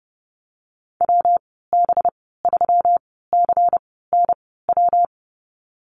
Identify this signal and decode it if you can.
Morse code, heard as a keyed tone, about 30 wpm, spaced as for 17 wpm, 700 Hz. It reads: WB3CNW
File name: Lesson 37g - Letter x - Call Signs 30wpm